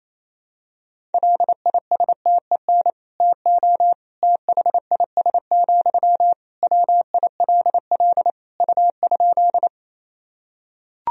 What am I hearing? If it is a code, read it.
Morse code, heard as a keyed tone, about 27 wpm, 700 Hz. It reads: LISTEN TO THIS, WILL U?
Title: Sentences from Top 400 Words and Abbreviations 28wpm